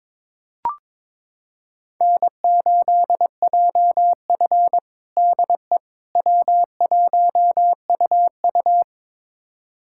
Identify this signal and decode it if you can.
Morse code, heard as a keyed tone, about 22 wpm, 700 Hz. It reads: N8JF DE W1UU
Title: USA Callsign DE Callsign - 4 Characters - 22wpm